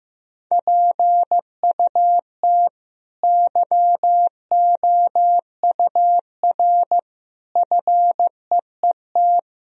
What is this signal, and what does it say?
Morse code, heard as a keyed tone, about 15 wpm, 700 Hz. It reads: PUT YOUR FEET
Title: Sentences from Top 1000 Words - Review Part 1 - 15wpm